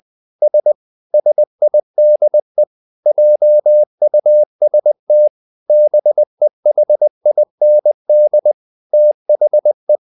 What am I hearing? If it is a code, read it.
Morse code, heard as a keyed tone, about 20 wpm, 600 Hz. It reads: S SIDE JUST BEHIND THE
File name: the old man and the sea part 2 20wpm